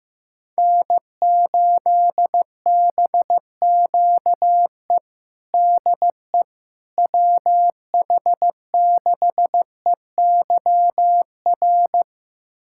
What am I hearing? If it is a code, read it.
Morse code, heard as a keyed tone, about 15 wpm, 700 Hz. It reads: N8BQE DE WH6EYR
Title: USA Callsign DE Callsign 15wpm